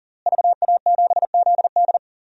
Morse code, heard as a keyed tone, about 40 wpm, 700 Hz. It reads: VA7ZD
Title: DXCC-100 Call Signs - 5 Characters - Repeated 3x - 40wpm